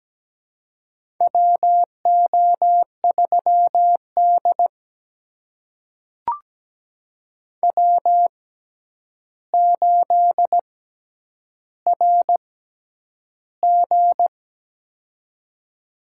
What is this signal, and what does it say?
Morse code, heard as a keyed tone, about 17 wpm, spaced as for 9 wpm, 700 Hz. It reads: WO3D W8RG
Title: USA Call Signs - 4 Characters - 6x Character Spacing - 17wpm